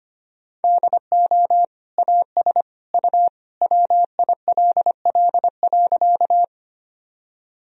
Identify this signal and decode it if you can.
Morse code, heard as a keyed tone, about 25 wpm, 700 Hz. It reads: DO AS U WILL.
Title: Sentences from Top 100 Words and Abbreviations 25wpm